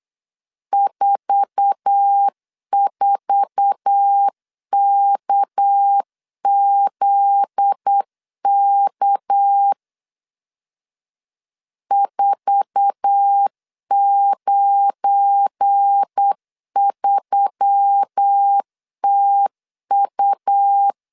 Morse code, heard as a keyed tone, about 8 wpm, 800 Hz. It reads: 44KZK 493TU
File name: LEZIONI / 8 YJ45 / 8 YJ4 8wpm 3246